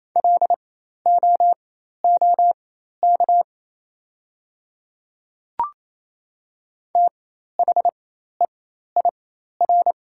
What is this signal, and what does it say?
Morse code, heard as a keyed tone, about 28 wpm, spaced as for 13 wpm, 700 Hz. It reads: LOOK THEIR
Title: 4x Character Spacing - Top 300 Words - 28wpm